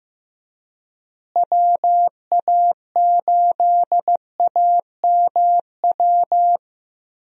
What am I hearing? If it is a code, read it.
Morse code, heard as a keyed tone, about 15 wpm, 700 Hz. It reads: WA8AMW
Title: USA Call Signs - 6 Characters - Repeated 3x - 15wpm